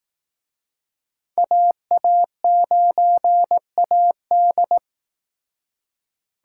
Morse code, heard as a keyed tone, about 19 wpm, 700 Hz. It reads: AA9AD
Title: Speed Racing - USA 2x2 Call Signs 15wpm